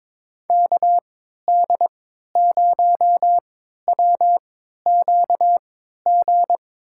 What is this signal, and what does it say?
Morse code, heard as a keyed tone, about 22 wpm, spaced as for 13 wpm, 700 Hz. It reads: KD0WQG
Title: USA Call Signs - 6 Characters - 3x Character Spacing - 22wpm